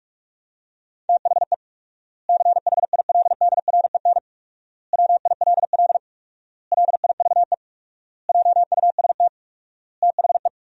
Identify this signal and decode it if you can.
Morse code, heard as a keyed tone, about 47 wpm, 700 Hz. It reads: THE CHILDREN WILL LIVE JUST THE
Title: 4x Word-Spacing - Sentences from Top 300 Words - 45wpm